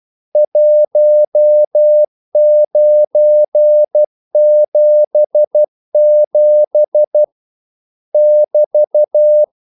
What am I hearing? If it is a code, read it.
Morse code, heard as a keyed tone, about 12 wpm, 600 Hz. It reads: 1977 <BT>
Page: English quotations in CW (telegraphy - morse)